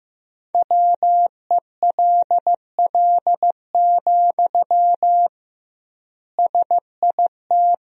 Morse code, heard as a keyed tone, about 15 wpm, 700 Hz. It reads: WELL, SIT
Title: 2x Word-Spacing - Sentences from Top 500 Words - 15wpm